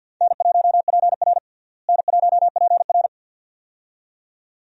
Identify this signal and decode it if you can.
Morse code, heard as a keyed tone, about 48 wpm, 700 Hz. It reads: N1PR N1PR
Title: USA Call Signs - 4 Characters - Repeated 2x - 50wpm